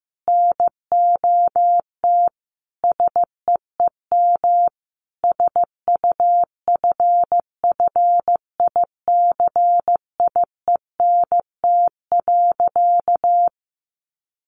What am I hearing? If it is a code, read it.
Morse code, heard as a keyed tone, about 15 wpm, 700 Hz. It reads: NOT SEEM SUFFICIENT.
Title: Sentences from Top 1400 Words 15wpm